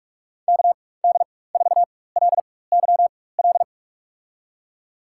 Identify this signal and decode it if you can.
Morse code, heard as a keyed tone, about 46 wpm, spaced as for 21 wpm, 700 Hz. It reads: KD4LYL
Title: USA Call Signs - 4x Character Spacing - 45wpm